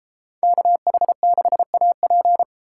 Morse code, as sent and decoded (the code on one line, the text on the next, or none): -.- .... -.... .- .--.
KH6AP